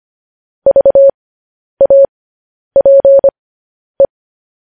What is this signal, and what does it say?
Morse code, heard as a keyed tone, about 25 wpm, spaced as for 10 wpm, 550 Hz. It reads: VAPE
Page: news headlines in morse code at 10 wpm